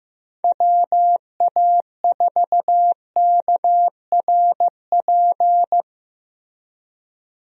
Morse code, heard as a keyed tone, about 15 wpm, 700 Hz. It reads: WA4KRP
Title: USA Call Signs - 6 Characters - Repeated 3x - 15wpm